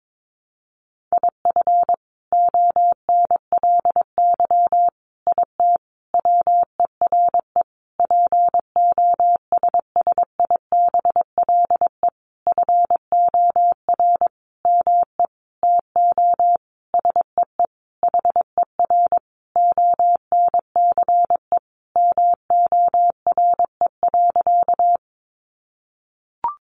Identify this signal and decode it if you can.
Morse code, heard as a keyed tone, about 22 wpm, 700 Hz. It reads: IF ONLY IT WERE POSSIBLE FOR ME TO SEE HER ONCE MORE.
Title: Sentences from Top 500 Words 22wpm